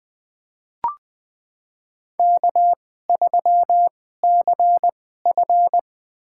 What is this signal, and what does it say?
Morse code, heard as a keyed tone, about 21 wpm, spaced as for 15 wpm, 700 Hz. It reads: K3CF
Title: USA Call Signs - 4 Characters - 2x Character Spacing - 20wpm